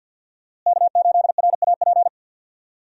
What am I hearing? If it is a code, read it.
Morse code, heard as a keyed tone, about 45 wpm, 700 Hz. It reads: K8RAP
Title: USA Call Signs - 5 Characters - Repeated 3x - 50wpm